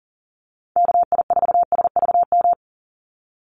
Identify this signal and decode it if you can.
Morse code, heard as a keyed tone, about 38 wpm, 700 Hz. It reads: KI4SVM
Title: NA SOTA Activator Callsigns 40wpm